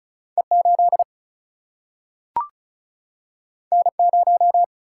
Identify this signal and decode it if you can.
Morse code, heard as a keyed tone, about 36 wpm, 700 Hz. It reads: E8 N0
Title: Single-Letter Single-Number 35wpm